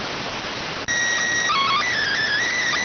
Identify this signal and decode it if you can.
SSTV (slow-scan television), PD 290.